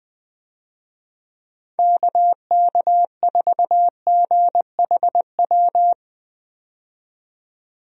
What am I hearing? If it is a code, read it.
Morse code, heard as a keyed tone, about 20 wpm, 700 Hz. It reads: KK4GHW
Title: USA Call Signs - 6 Characters - 20wpm